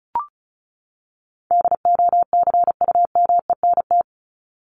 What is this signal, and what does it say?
Morse code, heard as a keyed tone, about 34 wpm, 700 Hz. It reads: DOCUMENT
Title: Top 1751-2000 Words 35wpm